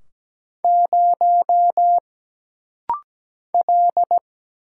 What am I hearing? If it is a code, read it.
Morse code, heard as a keyed tone, about 17 wpm, 700 Hz. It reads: 0 L
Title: Single Letter-Number Alt - Rapid-Fire 17wpm